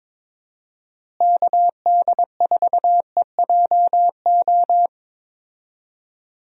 Morse code, heard as a keyed tone, about 22 wpm, 700 Hz. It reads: KD4EJO
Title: USA Call Signs - 6 Characters - Repeated 3x - 22wpm